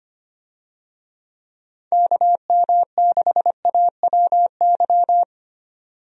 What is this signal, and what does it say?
Morse code, heard as a keyed tone, about 25 wpm, 700 Hz. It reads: KM6AWY
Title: USA Call Signs - 6 Characters - 25wpm